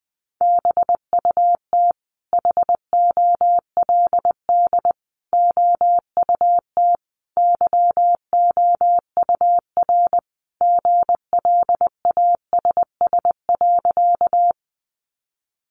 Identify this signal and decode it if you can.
Morse code, heard as a keyed tone, about 20 wpm, 700 Hz. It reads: BUT HOLD OUT YOUR GLASS.
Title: Sentences from Top 1300 Words 20wpm